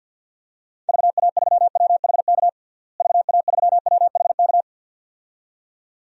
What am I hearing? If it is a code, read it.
Morse code, heard as a keyed tone, about 48 wpm, 700 Hz. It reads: VA3WHK VA3WHK
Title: DXCC-100 Call Signs - 6 Characters - Repeated 2x - 50wpm